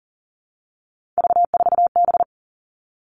Morse code, heard as a keyed tone, about 40 wpm, 700 Hz. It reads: V4B
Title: All Cut Numbers - 3-Digit 40wpm